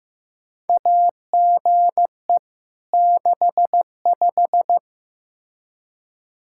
Morse code, heard as a keyed tone, about 15 wpm, 700 Hz. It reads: AGE 65
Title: K1USN SST QSO Practice 15wpm